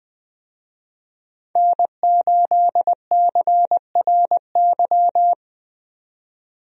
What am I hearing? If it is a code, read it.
Morse code, heard as a keyed tone, about 20 wpm, 700 Hz. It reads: N8CRY
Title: USA Call Signs - 5 Characters - 20wpm